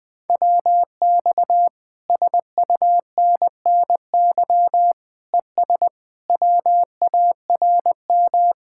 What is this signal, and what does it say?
Morse code, heard as a keyed tone, about 20 wpm, 700 Hz. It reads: WX SUNNY ES WARM
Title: Speed-Racing Common QSO Phrases Encore - 17wpm